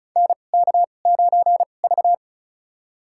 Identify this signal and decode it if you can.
Morse code, heard as a keyed tone, about 34 wpm, spaced as for 25 wpm, 700 Hz. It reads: NK9V
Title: USA Call Signs - 4 Characters - 2x Character Spacing - 35wpm